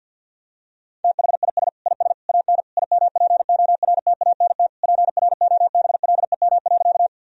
Morse code, heard as a keyed tone, about 52 wpm, 700 Hz. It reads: THIS IS AN IMPORTANT PROBLEM.
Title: Sentences from Top 1000 Words - Review Part 2 - 50wpm